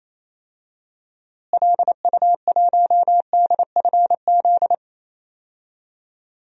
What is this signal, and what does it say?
Morse code, heard as a keyed tone, about 28 wpm, 700 Hz. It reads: LU1DFZ
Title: DXCC-100 Call Signs - 6 Characters - Repeated 2x - 28wpm